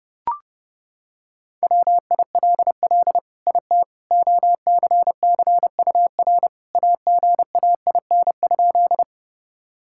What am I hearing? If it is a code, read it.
Morse code, heard as a keyed tone, about 32 wpm, 700 Hz. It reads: WILL IT OCCUR AGAIN?
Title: Sentences from Top 1000 Words - Review Part 6 - 30wpm